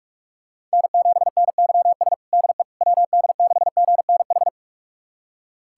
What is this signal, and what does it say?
Morse code, heard as a keyed tone, about 41 wpm, 700 Hz. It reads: N7NYS DE WD6GNH
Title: USA Callsign DE Callsign 45wpm